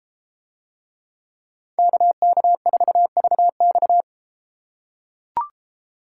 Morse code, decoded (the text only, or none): KK4VX